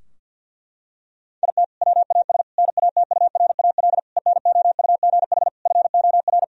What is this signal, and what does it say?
Morse code, heard as a keyed tone, about 50 wpm, 700 Hz. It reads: IT WAS NATURAL ENOUGH FOR
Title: Sentences from Top 1000 Words - Review Part 4 - 50wpm